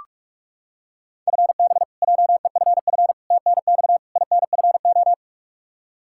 Morse code, heard as a keyed tone, about 46 wpm, 700 Hz. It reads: FB JEFF TNX INFO
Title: Practice QSO - Exchange 3 - More Info - 45wpm